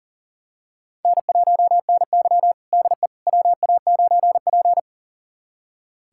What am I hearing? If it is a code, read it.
Morse code, heard as a keyed tone, about 40 wpm, 700 Hz. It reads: N1NY DE WA9P
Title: USA Callsign DE Callsign - 4 Characters - 40wpm